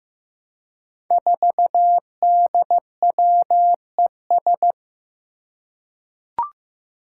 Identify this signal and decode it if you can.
Morse code, heard as a keyed tone, about 15 wpm, 700 Hz. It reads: 4DWES
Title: Five Challenging Letter-Number Combinations 15wpm